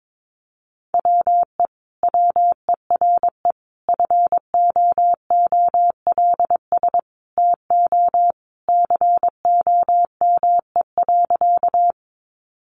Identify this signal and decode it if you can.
Morse code, heard as a keyed tone, about 22 wpm, 700 Hz. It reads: WE WERE FOOLS TO COME.
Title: A Sound Of Thunder 22wpm